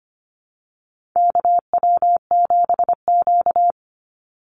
Morse code, heard as a keyed tone, about 25 wpm, 700 Hz. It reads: KW7Q
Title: Call Signs Repeated 2x - 25wpm